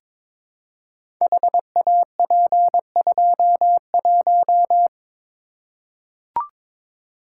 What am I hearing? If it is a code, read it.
Morse code, heard as a keyed tone, about 22 wpm, 700 Hz. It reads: HAP21
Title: Code Groups - 5 Letter:Numbers - 22wpm